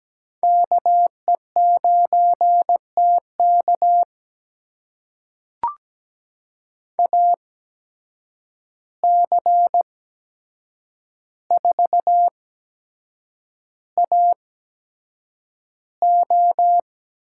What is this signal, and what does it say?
Morse code, heard as a keyed tone, about 17 wpm, spaced as for 7 wpm, 700 Hz. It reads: KE9TK AC4AO
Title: USA Call Signs - 5 Characters - 8x Character Spacing - 17wpm